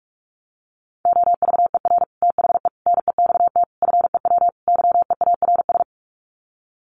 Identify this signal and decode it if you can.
Morse code, heard as a keyed tone, about 44 wpm, 700 Hz. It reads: OVER THE NEXT FEW YEARS
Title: Sets of 5 Words Encore 45wpm